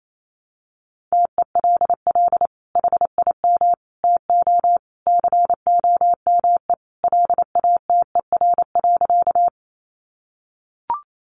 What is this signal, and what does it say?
Morse code, heard as a keyed tone, about 28 wpm, 700 Hz. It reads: TELL HIM TO COME LATER.
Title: Sentences from Top 700 Words 28wpm